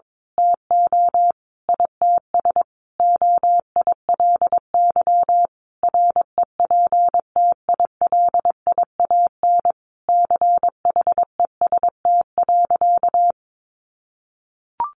Morse code, heard as a keyed tone, about 22 wpm, 700 Hz. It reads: TO ITS OILY REPTILIAN CHEST.
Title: A Sound Of Thunder 22wpm